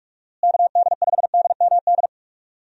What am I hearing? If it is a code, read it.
Morse code, heard as a keyed tone, about 41 wpm, 700 Hz. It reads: KD5DMD
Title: USA Call Signs - 6 Characters - Repeated 2x - 45wpm